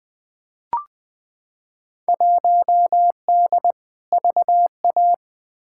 Morse code, heard as a keyed tone, about 20 wpm, 700 Hz. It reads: 1D VA
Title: Field Day Exchanges - 20wpm